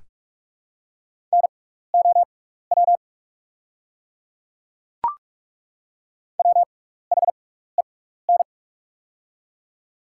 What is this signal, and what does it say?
Morse code, heard as a keyed tone, about 43 wpm, spaced as for 15 wpm, 690 Hz. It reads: NOW WHEN
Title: 6x Character Spacing - Top 100 Words - 45wpm